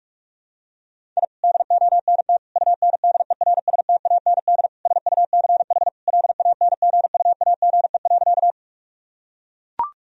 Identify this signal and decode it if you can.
Morse code, heard as a keyed tone, about 42 wpm, 700 Hz. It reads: I DONT UNDERSTAND SUCH LANGUAGE.
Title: Sentences from Top 1000 Words - Review Part 3 - 45wpm